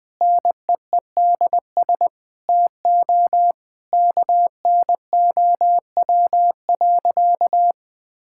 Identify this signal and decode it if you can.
Morse code, heard as a keyed tone, about 21 wpm, 700 Hz. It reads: NEEDS TO KNOW.
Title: Sentences with 2-Word Phrases - 20wpm